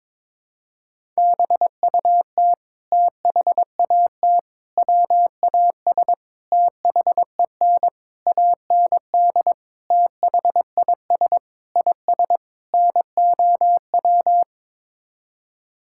Morse code, heard as a keyed tone, about 22 wpm, 700 Hz. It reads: BUT THAT WAS THEN AND THIS IS NOW
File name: practice / 22wpm / Sets of 8 Words - 22wpm